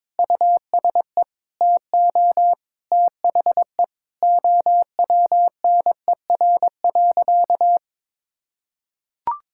Morse code, heard as a keyed tone, about 22 wpm, 700 Hz. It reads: USE TO THE OWNER.
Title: Sentences from Top 1600 Words 22wpm